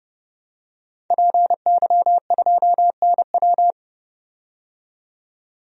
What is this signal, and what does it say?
Morse code, heard as a keyed tone, about 30 wpm, 700 Hz. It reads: PY2NW
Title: DXCC-100 Call Signs - 5 Characters - 30wpm